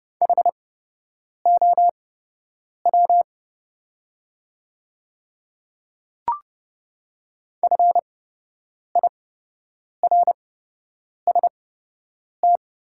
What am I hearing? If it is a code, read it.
Morse code, heard as a keyed tone, about 30 wpm, spaced as for 8 wpm, 700 Hz. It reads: HOW FIRST
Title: 8x Character Spacing - Top 200 Words - 30wpm